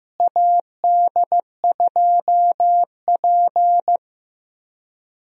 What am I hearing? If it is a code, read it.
Morse code, heard as a keyed tone, about 15 wpm, 700 Hz. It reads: AD2P
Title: USA Call Signs - 4 Characters - Repeated 3x - 15wpm